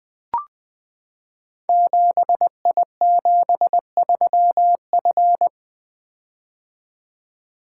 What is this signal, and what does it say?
Morse code, heard as a keyed tone, about 20 wpm, 700 Hz. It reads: 7I73F
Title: Code Groups - 5 Letter:Numbers - 20wpm